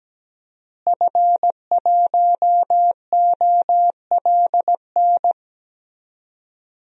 Morse code, heard as a keyed tone, about 17 wpm, 700 Hz. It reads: F1OLN
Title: DXCC-100 Call Signs - 5 Characters - Repeated 3x - 17wpm